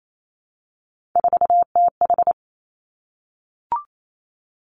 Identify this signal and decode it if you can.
Morse code, heard as a keyed tone, about 28 wpm, 700 Hz. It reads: 4TH
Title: Three Challenging Letter-Number Combinations 28wpm